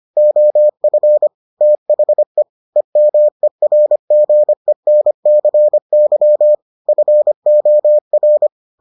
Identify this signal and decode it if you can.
Morse code, heard as a keyed tone, about 25 wpm, 600 Hz. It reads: OF THE EMERGENCY FOR